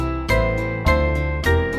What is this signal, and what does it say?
There is no code.